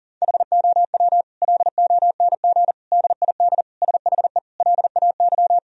Morse code, heard as a keyed tone, about 37 wpm, 700 Hz. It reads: HOW LONG DID SHE LAY
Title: Sentences from Top 1000 Words - Review Part 2 - 40wpm